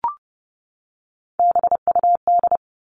Morse code, heard as a keyed tone, about 30 wpm, 700 Hz. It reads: BUD
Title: All Cut Numbers - 3-Digit 30wpm